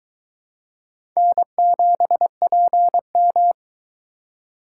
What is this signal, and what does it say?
Morse code, heard as a keyed tone, about 23 wpm, 700 Hz. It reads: N7PM